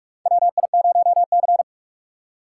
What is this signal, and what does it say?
Morse code, heard as a keyed tone, about 46 wpm, 700 Hz. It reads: WI0C
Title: USA Call Signs - 4 Characters - 4x Character Spacing - 45wpm